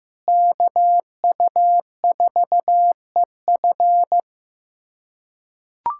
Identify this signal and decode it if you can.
Morse code, heard as a keyed tone, about 15 wpm, 700 Hz. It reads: KU4EF
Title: USA Callsign DE Callsign - 5 Characters - 15wpm